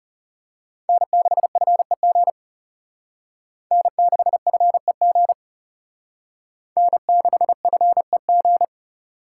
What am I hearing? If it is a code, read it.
Morse code, heard as a keyed tone, about 34 wpm, 700 Hz. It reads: N6FEG N6FEG N6FEG
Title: Speed Racing - USA 1x3 Call Signs 30wpm